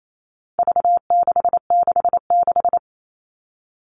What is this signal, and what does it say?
Morse code, heard as a keyed tone, about 28 wpm, 700 Hz. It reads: V666
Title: All Cut Numbers - 4-Digit 28wpm